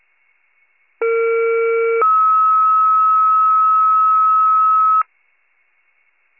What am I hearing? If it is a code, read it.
Two-tone page, heard as a one-second tone, then a three-second tone, about 460 then 1290 Hz. It is heard from 1.0 to 5.0 s.